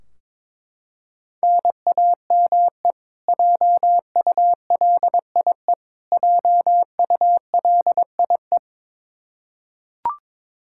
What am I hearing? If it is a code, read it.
Morse code, heard as a keyed tone, about 22 wpm, 700 Hz. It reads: NAME JULIE JULIE
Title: Practice QSO - Exchange 1 - Beginning - 22wpm